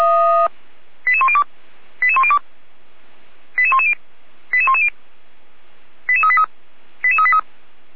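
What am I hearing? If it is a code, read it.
Five-tone selective call (ZVEI), heard as quick runs of five tones, tones of about 70 ms. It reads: 88102 88102 88110 88110 88383 88383
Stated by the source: ZVEI five-tone selective call 88103 received on an SDR, from the feuerwehr-orlamuende.org (fire brigade Orlamünde) recording server.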